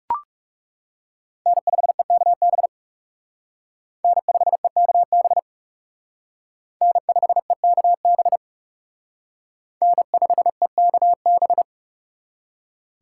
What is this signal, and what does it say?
Morse code, heard as a keyed tone, about 34 wpm, 700 Hz. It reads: N5EKB N5EKB N5EKB N5EKB
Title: Speed Racing - USA 1x3 Call Signs 30wpm